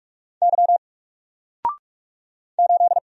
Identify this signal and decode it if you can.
Morse code, heard as a keyed tone, about 45 wpm, 700 Hz. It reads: Y 8
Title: Single Character (Letter:Number:Punctuation:Prosign) 45wpm